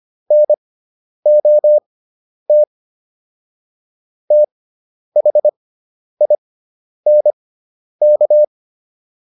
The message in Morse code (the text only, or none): NOT THINK